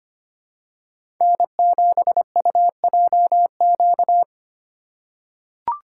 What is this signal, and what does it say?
Morse code, heard as a keyed tone, about 25 wpm, 700 Hz. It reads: N7UJQ
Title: USA Call Signs - 2x Character Spacing - 25wpm